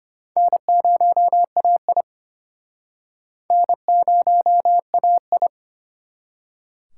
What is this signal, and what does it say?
Morse code, heard as a keyed tone, about 33 wpm, 700 Hz. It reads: N0AI N0AI